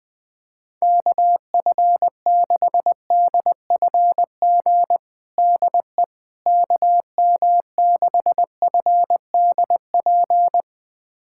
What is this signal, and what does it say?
Morse code, heard as a keyed tone, about 21 wpm, 700 Hz. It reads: KF6DFG DE KM6FDP
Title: USA Callsign DE Callsign - 6 Characters - 20wpm